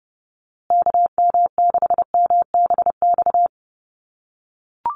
Morse code, heard as a keyed tone, about 30 wpm, 700 Hz. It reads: KM6MBX
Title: USA Call Signs Repeated 3x 30wpm